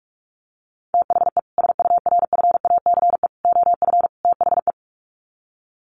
Morse code, heard as a keyed tone, about 44 wpm, 700 Hz. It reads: THE SURFACE OF THE
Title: Sets of 4 Words 45wpm